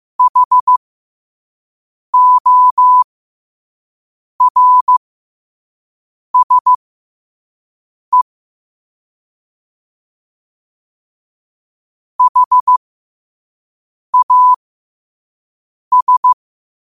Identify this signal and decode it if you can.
Morse code, heard as a keyed tone, about 15 wpm, spaced as for 5 wpm, 1000 Hz. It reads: HORSE HAS
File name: qotdfiles / 5wpm